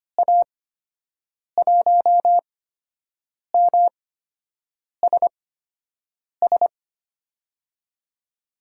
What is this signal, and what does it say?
Morse code, heard as a keyed tone, about 25 wpm, spaced as for 7 wpm, 700 Hz. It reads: A1MSS
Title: USA Call Signs - 6 Characters - 8x Character Spacing - 25wpm